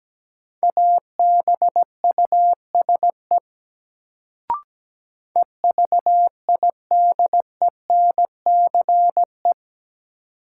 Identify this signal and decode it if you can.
Morse code, heard as a keyed tone, about 17 wpm, 700 Hz. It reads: ABUSE EVIDENCE
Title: Sentences from Top 1300 Words - New Word Drill Alt - 17wpm